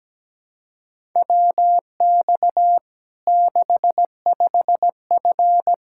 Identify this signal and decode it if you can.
Morse code, heard as a keyed tone, about 17 wpm, 700 Hz. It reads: WX 65F
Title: Practice QSO - Exchange 2 - Info - 17wpm